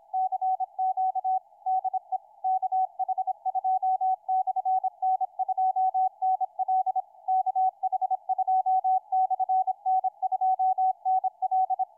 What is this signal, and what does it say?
Morse code, heard as a keyed tone, about 26 wpm, 750 Hz. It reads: CQ DE KH2/N2NL KH2/N2NL
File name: kh2n2nl 30-12-13 1241ut 160m